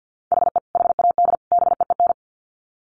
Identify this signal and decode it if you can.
Morse code, heard as a keyed tone, about 50 wpm, 700 Hz. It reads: HE HAD BEEN
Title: Sets of 3 Words 50wpm